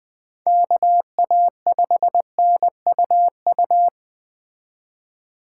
Morse code, heard as a keyed tone, about 20 wpm, 700 Hz. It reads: KA5NUU
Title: Speed Racing - USA Call Signs 17wpm